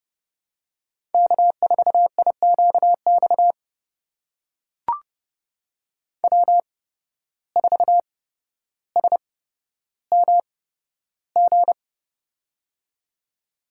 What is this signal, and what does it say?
Morse code, heard as a keyed tone, about 30 wpm, spaced as for 13 wpm, 700 Hz. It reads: K4IQX W4SMG
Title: USA Call Signs - 5 Characters - 8x Character Spacing - 30wpm